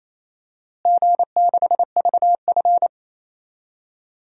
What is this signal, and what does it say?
Morse code, heard as a keyed tone, about 28 wpm, 700 Hz. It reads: G6VF
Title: DXCC-100 Call Signs - 4 Characters - Repeated 3x - 28wpm